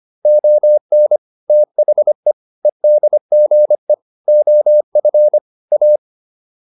Morse code, heard as a keyed tone, about 25 wpm, 600 Hz. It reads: ON THE EDGE OF A